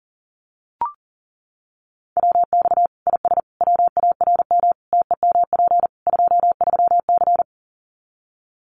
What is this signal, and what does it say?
Morse code, heard as a keyed tone, about 40 wpm, 700 Hz. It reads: WX IS WARM TEMP 23C
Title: Common QSO Phrases 40wpm